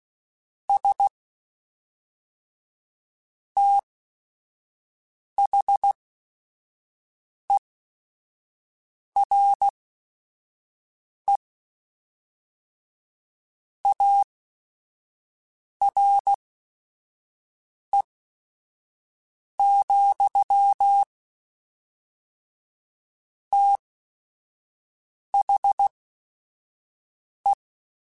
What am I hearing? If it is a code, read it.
Morse code, heard as a keyed tone, about 16 wpm, spaced as for 5 wpm, 800 Hz. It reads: S THERE ARE, THE